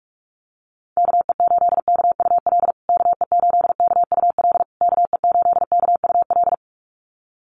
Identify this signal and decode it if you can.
Morse code, heard as a keyed tone, about 44 wpm, 700 Hz. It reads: KE8KUL KE8KUL KE8KUL
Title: USA Call Signs Repeated 3x 45wpm